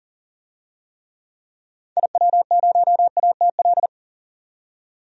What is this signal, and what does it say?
Morse code, heard as a keyed tone, about 40 wpm, 700 Hz. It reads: IW0ATL
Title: DXCC-100 Call Signs - 6 Characters - 40wpm